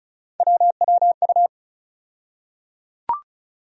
Morse code, heard as a keyed tone, about 34 wpm, 700 Hz. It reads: WWU
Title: USA Call Sign Postfixes - 3 Letters 35wpm